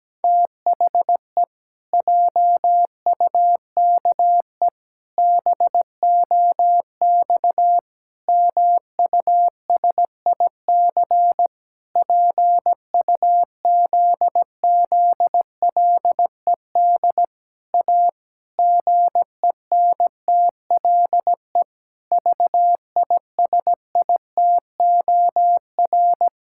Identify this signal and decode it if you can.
Morse code, heard as a keyed tone, about 17 wpm, 700 Hz. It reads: THE JUKE BOX MUSIC PUZZLED A GENTLE VISITOR
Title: Pangram Challenge v2 - 17wpm